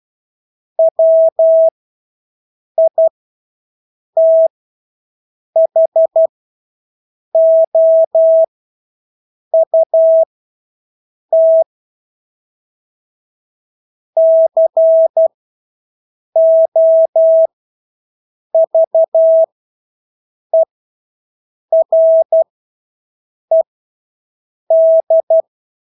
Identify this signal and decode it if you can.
Morse code, heard as a keyed tone, about 12 wpm, spaced as for 6 wpm, 650 Hz. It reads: WITHOUT COVERED
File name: Engelske ord 12-6 wpm